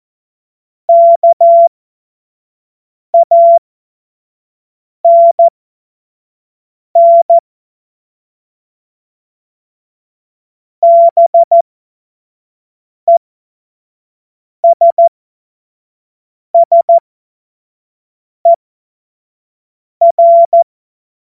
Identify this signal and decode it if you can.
Morse code, heard as a keyed tone, about 14 wpm, spaced as for 5 wpm, 680 Hz. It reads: KANN BESSER
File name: Witz-14-Oktober-2025-25-F70